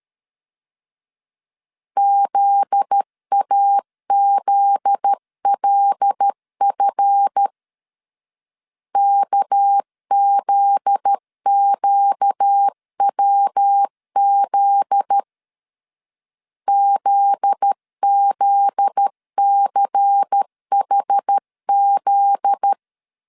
Morse code, heard as a keyed tone, about 13 wpm, 800 Hz. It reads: ZAZLF KZQWZ ZZCHZ
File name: LEZIONI / 7 GF9Z / 7 GF9Z 12wpm 2147